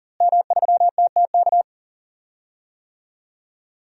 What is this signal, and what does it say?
Morse code, heard as a keyed tone, about 39 wpm, 700 Hz. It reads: M3TTK
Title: DXCC-100 Call Signs - 5 Characters - Repeated 2x - 40wpm